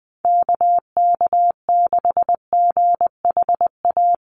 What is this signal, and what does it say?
Morse code, heard as a keyed tone, about 19 wpm, 700 Hz. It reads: KK6GHA
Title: Call Signs - 20wpm